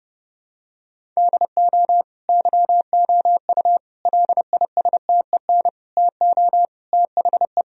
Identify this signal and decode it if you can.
Morse code, heard as a keyed tone, about 32 wpm, 700 Hz. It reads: DO YOU LISTEN TO THE
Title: Sentences from Top 1000 Words - Review Part 3 - 30wpm